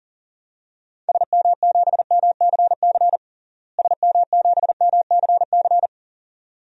Morse code, heard as a keyed tone, about 40 wpm, 700 Hz. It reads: SM7MCC SM7MCC
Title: DXCC-100 Call Signs - 6 Characters - Repeated 2x - 40wpm